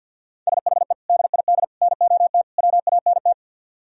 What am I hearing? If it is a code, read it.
Morse code, heard as a keyed tone, about 62 wpm, 700 Hz. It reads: SHE DID NOT WANT
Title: Speed-Racing - Sets of 4 Words 50wpm